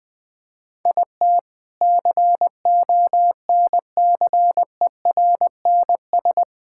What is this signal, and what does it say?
Morse code, heard as a keyed tone, about 20 wpm, 700 Hz. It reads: IT CONCERNS
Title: Sentences with 3-Word Phrases - 20wpm